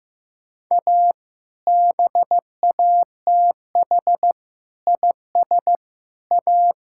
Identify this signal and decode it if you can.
Morse code, heard as a keyed tone, about 15 wpm, 700 Hz. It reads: A BATH IS A
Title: Sentences from Top 1600 Words 15wpm